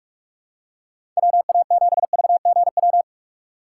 Morse code, heard as a keyed tone, about 42 wpm, 700 Hz. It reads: WA7VGW
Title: USA Call Signs - 6 Characters - 3x Character Spacing - 45wpm